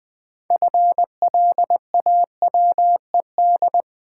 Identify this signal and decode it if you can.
Morse code, heard as a keyed tone, about 20 wpm, 700 Hz. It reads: FLAWED
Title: 6 Letter Words - First 3 Letters Given - 20wpm